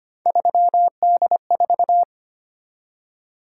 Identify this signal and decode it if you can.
Morse code, heard as a keyed tone, about 25 wpm, 700 Hz. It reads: 3D4
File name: Three Letter-Number Combinations - 25wpm